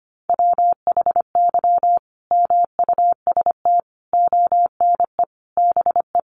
Morse code, heard as a keyed tone, about 25 wpm, 700 Hz. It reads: WHY MUST ONE BE
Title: Sentences from Top 1400 Words 25wpm